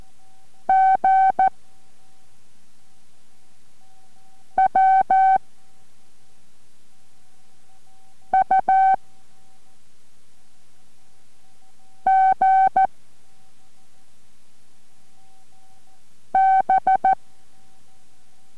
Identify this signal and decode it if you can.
Morse code, heard as a keyed tone, about 14 wpm, 770 Hz. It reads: G W U G B